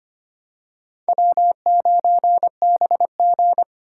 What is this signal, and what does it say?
Morse code, heard as a keyed tone, about 25 wpm, 700 Hz. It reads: W9BG
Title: USA Call Signs - 4 Characters - 2x Character Spacing - 25wpm